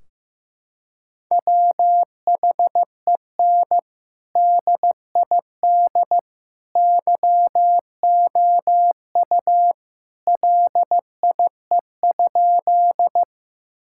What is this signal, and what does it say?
Morse code, heard as a keyed tone, about 15 wpm, 700 Hz. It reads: WHEN DID YOU LIE?
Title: Sentences from Top 1600 Words 15wpm